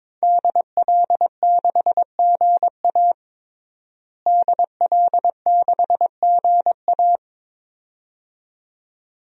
Morse code, heard as a keyed tone, about 21 wpm, 700 Hz. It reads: DL6GA DL6GA
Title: DXCC-100 Call Signs - 5 Characters - Repeated 2x - 22wpm